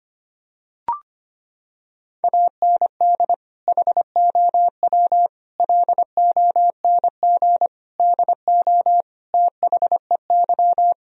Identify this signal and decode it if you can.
Morse code, heard as a keyed tone, about 25 wpm, 700 Hz. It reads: AND HOW LONG DO THEY
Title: Sentences from Top 1000 Words - Review Part 4 - 25wpm